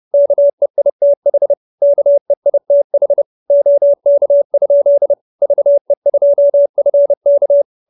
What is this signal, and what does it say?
Morse code, heard as a keyed tone, about 30 wpm, 570 Hz. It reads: KEITH KEITH OK? VE2FK